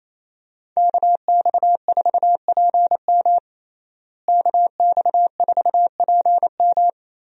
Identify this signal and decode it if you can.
Morse code, heard as a keyed tone, about 28 wpm, 700 Hz. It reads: KX4PM KX4PM
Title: USA Call Signs - 5 Characters - Repeated 3x - 28wpm